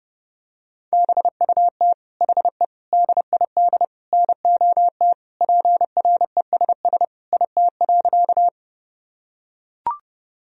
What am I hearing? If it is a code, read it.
Morse code, heard as a keyed tone, about 30 wpm, 700 Hz. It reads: BUT HE DID NOT PRESS IT.
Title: Sentences from Top 1000 Words - Review Part 1 - 30wpm